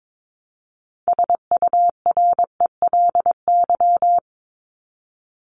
Morse code, heard as a keyed tone, about 22 wpm, 700 Hz. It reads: SURELY
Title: LY Suffix - Top 100 Words with Base Words 22wpm